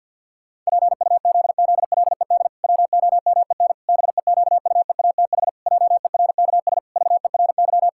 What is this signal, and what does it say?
Morse code, heard as a keyed tone, about 50 wpm, 700 Hz. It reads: PUZZLED WOMEN BEQUEATH JERKS VERY
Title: Pangram Challenge v2 - 50wpm